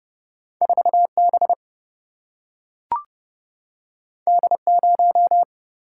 Morse code, heard as a keyed tone, about 30 wpm, 700 Hz. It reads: 4B D0
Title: Single-Letter Single-Number 30wpm